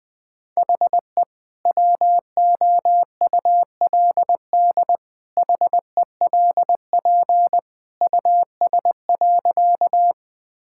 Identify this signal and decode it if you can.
Morse code, heard as a keyed tone, about 20 wpm, 700 Hz. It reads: HE WOULD HELP US.
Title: Sentences with 2-Word Phrases - Spoken Beginning - 20wpm